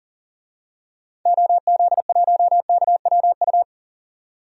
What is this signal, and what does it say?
Morse code, heard as a keyed tone, about 38 wpm, 700 Hz. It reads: OZ1KWU
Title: DXCC-100 Call Signs - 6 Characters - 40wpm